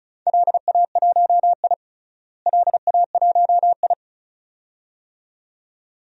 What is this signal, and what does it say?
Morse code, heard as a keyed tone, about 36 wpm, 700 Hz. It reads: LA1I LA1I
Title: DXCC-100 Call Signs - 4 Characters - Repeated 2x - 35wpm